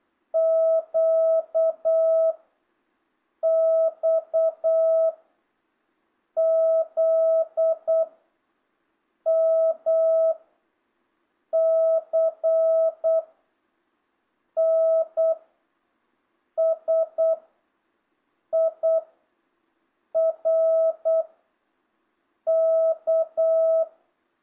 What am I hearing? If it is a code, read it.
Morse code, heard as a keyed tone, about 8 wpm, 660 Hz. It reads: Q X Z M C N S I R K
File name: jh8cbh / 2025cw